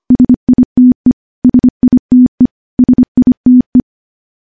Morse code, heard as a keyed tone, about 25 wpm, 250 Hz. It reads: SITE SITE SITE
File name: morse-training-25WPM